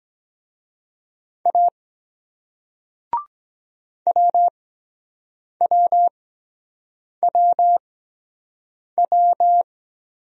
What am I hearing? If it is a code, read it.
Morse code, heard as a keyed tone, about 23 wpm, spaced as for 7 wpm, 700 Hz. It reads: A WWWW